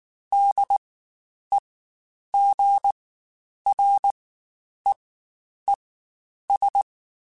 Morse code, heard as a keyed tone, about 19 wpm, spaced as for 9 wpm, 800 Hz. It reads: DEGREES
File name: Spaced Characters / Spaced Characters-mp3s / Weather5 5 sp @18